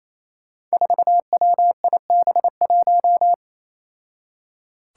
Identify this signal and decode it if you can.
Morse code, heard as a keyed tone, about 29 wpm, 700 Hz. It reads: 4WIB1
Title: Five Challenging Letter-Number Combinations 28wpm